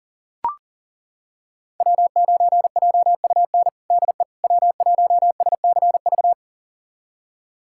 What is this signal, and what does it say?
Morse code, heard as a keyed tone, about 37 wpm, 690 Hz. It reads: W9JUN DE W1SCV
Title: USA Callsign DE Callsign - 5 Characters - 40wpm